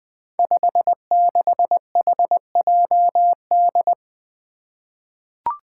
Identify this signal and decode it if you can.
Morse code, heard as a keyed tone, about 21 wpm, 700 Hz. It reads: 56HJD
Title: Five Challenging Letter-Number Combinations 20wpm